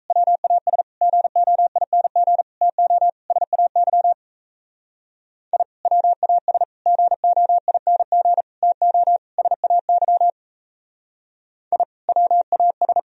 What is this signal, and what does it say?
Morse code, heard as a keyed tone, about 38 wpm, 700 Hz. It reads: WAS GOING TO SAY I WAS GOING TO SAY I WAS